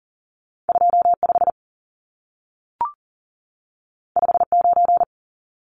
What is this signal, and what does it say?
Morse code, heard as a keyed tone, about 40 wpm, 700 Hz. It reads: 25 59
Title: Numbers - 2-Digit 40wpm